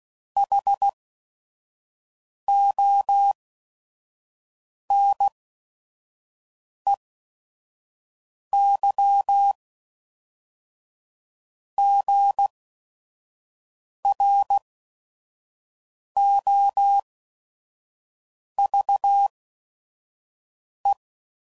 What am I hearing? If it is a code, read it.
Morse code, heard as a keyed tone, about 16 wpm, spaced as for 5 wpm, 800 Hz. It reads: HONEYGROVE
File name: CW Code Groups @5WPM / fromSM's / Cities